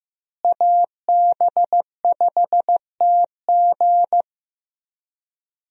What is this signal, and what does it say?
Morse code, heard as a keyed tone, about 15 wpm, 700 Hz. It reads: AB5TG
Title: USA Call Signs - 5 Characters - Repeated 3x - 15wpm